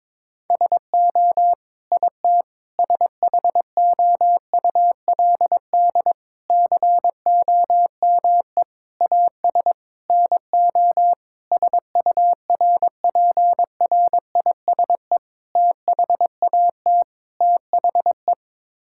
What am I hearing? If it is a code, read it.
Morse code, heard as a keyed tone, about 22 wpm, 700 Hz. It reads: SO IT SHOULD COME AS NO SURPRISE THAT THE
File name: Sets of 9 Words - 22wpm